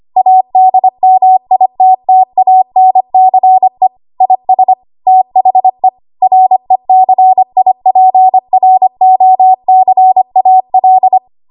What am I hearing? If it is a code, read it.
Morse code, heard as a keyed tone, about 25 wpm, 750 Hz. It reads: ADMITTANCE IS THE RECIPROCAL
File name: Archive / 25 WPM / 25WPM